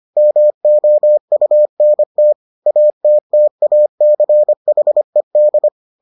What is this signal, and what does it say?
Morse code, heard as a keyed tone, about 25 wpm, 600 Hz. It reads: MOUNT ATTACHED